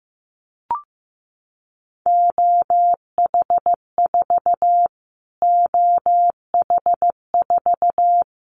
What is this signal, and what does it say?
Morse code, heard as a keyed tone, about 15 wpm, 700 Hz. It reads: OH4 OH4
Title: Callsign Prefixes for Top 25 DXCC - Repeated 3x - 15wpm